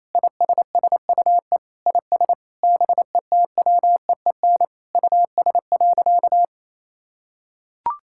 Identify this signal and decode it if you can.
Morse code, heard as a keyed tone, about 27 wpm, 700 Hz. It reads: ISSUE IS BETWEEN US.
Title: Sentences from Top 1000 Words - Review Part 6 - 28wpm